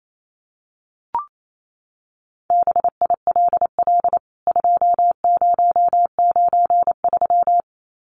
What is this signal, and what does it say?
Morse code, heard as a keyed tone, about 28 wpm, 700 Hz. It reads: BILL 2093
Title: CWT Exchange 28wpm